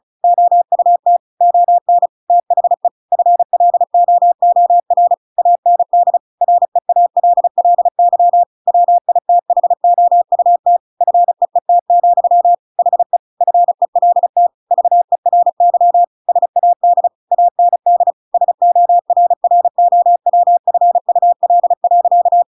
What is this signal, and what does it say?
Morse code, heard as a keyed tone, about 34 wpm, 700 Hz. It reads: OUT ON THE FLOOR AND REALLY WITHOUT FEET, HE FELT VERY SAD AND SORROWFUL.
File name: Pinocchio / Pinocchio ch7 35 35 ik1hge